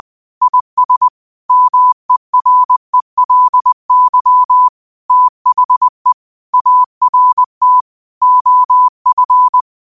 Morse code, heard as a keyed tone, about 20 wpm, 1000 Hz. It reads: IS MERELY THE ART OF
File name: qotdfiles / 20wpm